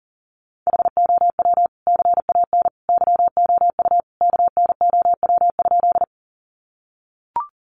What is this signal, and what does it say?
Morse code, heard as a keyed tone, about 40 wpm, 700 Hz. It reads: HOW CAN YOU KNOW?
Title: Sentences from Top 100 Words 40wpm